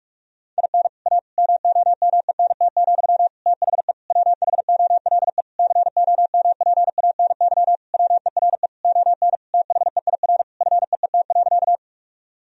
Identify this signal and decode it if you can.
Morse code, heard as a keyed tone, about 46 wpm, 700 Hz. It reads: IN A MOMENT, THE WHOLE COMPANY WERE ON THEIR FEET.
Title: Sentences from Top 1000 Words - Review Part 3 - 45wpm